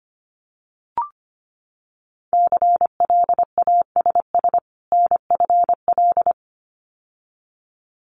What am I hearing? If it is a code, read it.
Morse code, heard as a keyed tone, about 25 wpm, 700 Hz. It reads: CLASS NFL
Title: Winter Field Day Exchanges 25wpm